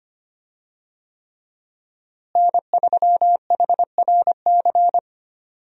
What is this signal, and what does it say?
Morse code, heard as a keyed tone, about 25 wpm, 700 Hz. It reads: N3HRC
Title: USA Call Signs - 5 Characters - Repeated 3x - 25wpm